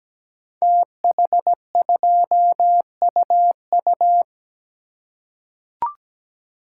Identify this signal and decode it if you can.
Morse code, heard as a keyed tone, about 17 wpm, 700 Hz. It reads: TH2UU
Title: Code Groups - 5 Letter:Numbers - 17wpm